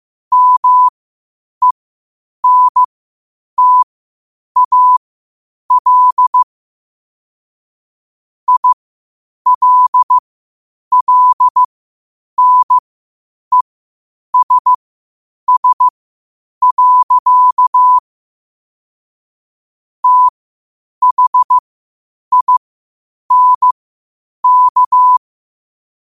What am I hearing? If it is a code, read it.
Morse code, heard as a keyed tone, about 15 wpm, spaced as for 8 wpm, 1000 Hz. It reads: MENTAL ILLNESS. THINK
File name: qotdfiles / 8wpm